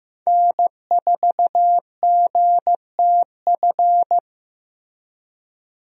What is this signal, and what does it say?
Morse code, heard as a keyed tone, about 15 wpm, 700 Hz. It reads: N4GTF
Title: USA Call Signs - 5 Characters - Repeated 2x - 15wpm